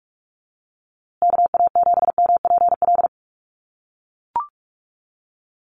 Morse code, heard as a keyed tone, about 44 wpm, 690 Hz. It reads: KA7MPL
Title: USA Call Signs Repeated 2x 45wpm